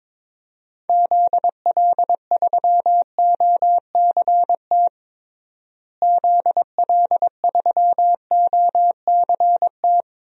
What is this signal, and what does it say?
Morse code, heard as a keyed tone, about 22 wpm, 700 Hz. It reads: ZL3OCT ZL3OCT
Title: DXCC-100 Call Signs - 6 Characters - Repeated 3x - 22wpm